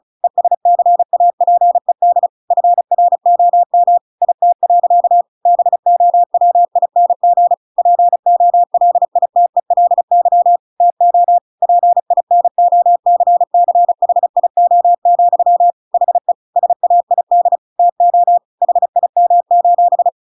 Morse code, heard as a keyed tone, about 34 wpm, 700 Hz. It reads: ESCAPED FROM IT. BOWING POLITELY TO PINOCCHIO, HE SAID TO HIM: